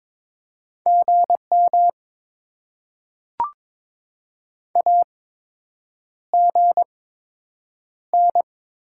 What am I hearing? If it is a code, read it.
Morse code, heard as a keyed tone, about 22 wpm, spaced as for 8 wpm, 700 Hz. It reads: GM AGN
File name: Top 20 QSO Elements - 8x Character Spacing - 22wpm